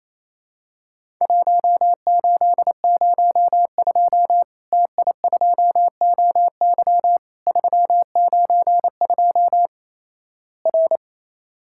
Morse code, heard as a keyed tone, about 28 wpm, 700 Hz. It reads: 1802 TI2OY 392 R
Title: Worked All Europe DX Contest - QTC Exchange - 28wpm